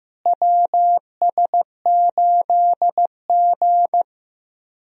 Morse code, heard as a keyed tone, about 15 wpm, 700 Hz. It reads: WS8G
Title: USA Call Signs - 4 Characters - Repeated 3x - 15wpm